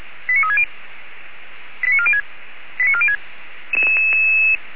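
Five-tone selective call (ZVEI), heard as quick runs of five tones, tones of about 70 ms. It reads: 89360 89406 89406
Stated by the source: ZVEI five-tone call received by SDR, logged on the audio server of Feuerwehr Orlamünde.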